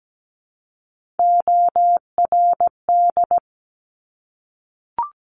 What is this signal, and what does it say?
Morse code, heard as a keyed tone, about 17 wpm, 700 Hz. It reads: ORD
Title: Top Three Letter Combinations in Words 17wpm